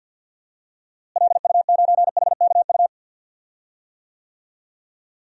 Morse code, heard as a keyed tone, about 52 wpm, 700 Hz. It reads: LU9HKU
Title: DXCC-100 Call Signs - 6 Characters - Repeated 3x - 50wpm